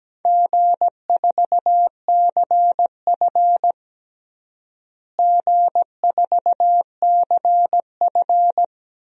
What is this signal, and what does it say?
Morse code, heard as a keyed tone, about 17 wpm, 700 Hz. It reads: G4CF G4CF
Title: DXCC-100 Call Signs - 4 Characters - Repeated 3x - 17wpm